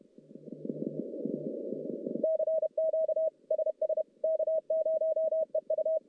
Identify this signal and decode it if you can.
Morse code, heard as a keyed tone, about 39 wpm, 630 Hz. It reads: CQ SS K0EU
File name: contests / CW-SS-k5tr- -1200